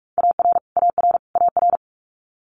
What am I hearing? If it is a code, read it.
Morse code, heard as a keyed tone, about 46 wpm, 700 Hz. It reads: AR AR AR
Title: Multi-letter Phonemes Repeated 3x 45wpm